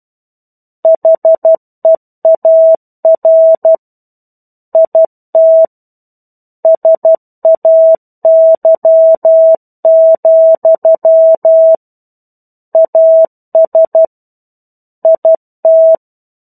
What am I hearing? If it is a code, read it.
Morse code, heard as a keyed tone, about 12 wpm, 650 Hz. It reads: HEAR IT SAY, AS IT
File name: 01-Chapter-01 12 WPM